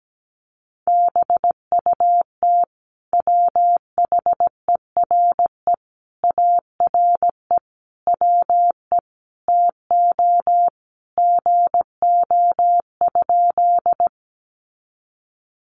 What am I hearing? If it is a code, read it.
Morse code, heard as a keyed tone, about 17 wpm, 700 Hz. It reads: BUT WHERE ARE WE TO GO?
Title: Sentences from Top 200 Words 17wpm